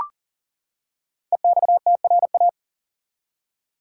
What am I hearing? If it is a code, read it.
Morse code, heard as a keyed tone, about 40 wpm, 700 Hz. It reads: EXTRA
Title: Five Letter Words - 40wpm